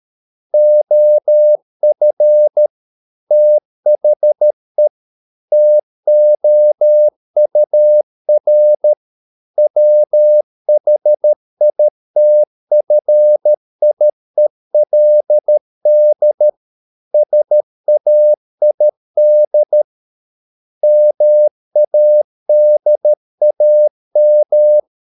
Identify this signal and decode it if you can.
Morse code, heard as a keyed tone, about 13 wpm, 600 Hz. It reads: OF THE TOUR WHITFIELD SAID MADAM